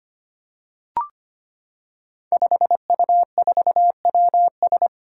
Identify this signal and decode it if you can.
Morse code, heard as a keyed tone, about 25 wpm, 700 Hz. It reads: 5U4WS
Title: Five Challenging Letter-Number Combinations 25wpm